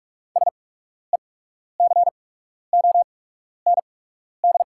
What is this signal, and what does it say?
Morse code, heard as a keyed tone, about 48 wpm, spaced as for 12 wpm, 690 Hz. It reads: SECOND
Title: 8x Character Spacing - Top 500 Words - 45wpm